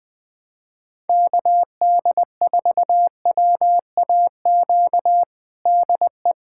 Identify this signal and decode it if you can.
Morse code, heard as a keyed tone, about 20 wpm, 700 Hz. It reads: KD4WAQ DE
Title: USA Callsign DE Callsign 20wpm